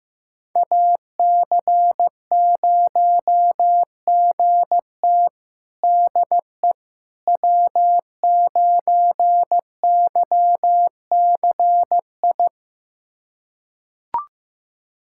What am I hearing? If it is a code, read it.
Morse code, heard as a keyed tone, about 15 wpm, 700 Hz. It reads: AC0GT DE W9YCI
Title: USA Callsign DE Callsign - 5 Characters - 15wpm